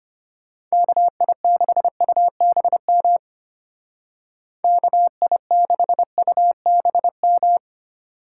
Morse code, heard as a keyed tone, about 32 wpm, 700 Hz. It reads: KI6UBM KI6UBM